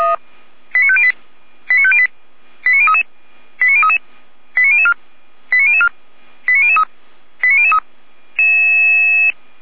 Five-tone selective call (ZVEI), heard as quick runs of five tones, tones of about 70 ms. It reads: 89508 89508 89033 89033 89004 89004 89003 89003